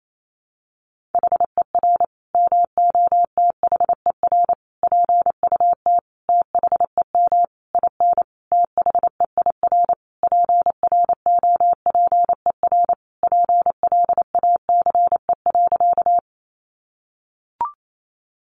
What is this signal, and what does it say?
Morse code, heard as a keyed tone, about 28 wpm, 700 Hz. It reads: HER MOTHER PUT THEM IN THEIR PROPER PLACE.
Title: Sentences from Top 800 Words 28wpm